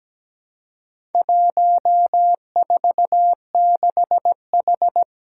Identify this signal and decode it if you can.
Morse code, heard as a keyed tone, about 17 wpm, 700 Hz. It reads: 146H
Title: Four Challenging Letter-Number Combinations 17wpm